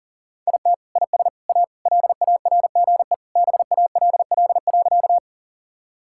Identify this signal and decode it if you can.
Morse code, heard as a keyed tone, about 37 wpm, 700 Hz. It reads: IT IS A LARGE BALL.
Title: Sentences from Top 1000 Words - Review Part 6 - 40wpm